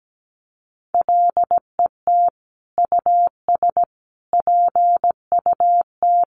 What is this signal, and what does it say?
Morse code, heard as a keyed tone, about 17 wpm, 700 Hz. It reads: LET US PUT
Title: Sentences from Top 1400 Words 17wpm